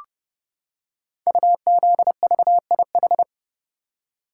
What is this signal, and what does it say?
Morse code, heard as a keyed tone, about 30 wpm, 700 Hz. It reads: UZVIH
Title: Code Groups - 5 Letters - 30wpm